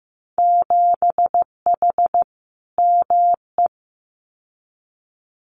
Morse code, heard as a keyed tone, about 15 wpm, 700 Hz. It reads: 7H ME
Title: Winter Field Day Exchanges Encore 15wpm